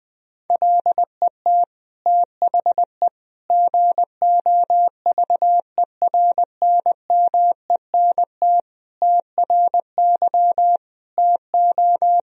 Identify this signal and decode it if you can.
Morse code, heard as a keyed tone, about 20 wpm, 700 Hz. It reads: LET THE GOVERNMENT TRY TO
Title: Sentences from Top 1000 Words - Review Part 3 - 20wpm